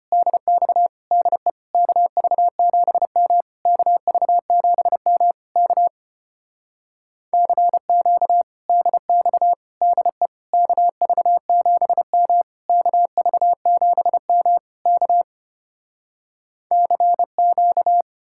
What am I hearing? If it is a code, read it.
Morse code, heard as a keyed tone, about 33 wpm, 700 Hz. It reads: DX DE KV7M KV7M K CQ DX DE KV7M KV7M K CQ